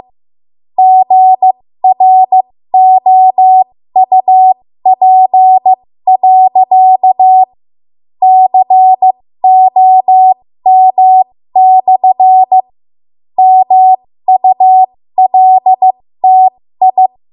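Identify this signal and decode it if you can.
Morse code, heard as a keyed tone, about 15 wpm, spaced as for 12 wpm, 750 Hz. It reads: GROUP.COM/MULTI
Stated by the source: 13 wpm Morse code practice archive transmission.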